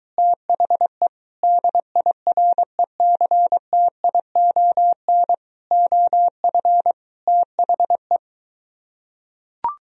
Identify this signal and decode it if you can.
Morse code, heard as a keyed tone, about 23 wpm, 700 Hz. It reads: THE DIRECTION OF THE